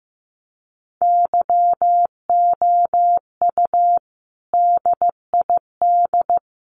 Morse code, heard as a keyed tone, about 15 wpm, 700 Hz. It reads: YOU DID
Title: Sentences from Top 900 Words 15wpm